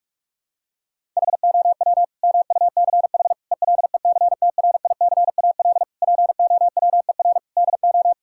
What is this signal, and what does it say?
Morse code, heard as a keyed tone, about 46 wpm, 700 Hz. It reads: HOW MUCH ELECTRICAL POWER DO
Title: Sentences from Top 1000 Words - Review Part 6 - 45wpm